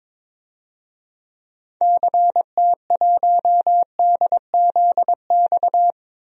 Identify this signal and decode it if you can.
Morse code, heard as a keyed tone, about 22 wpm, 700 Hz. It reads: CT1DZX
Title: DXCC-100 Call Signs - 6 Characters - 22wpm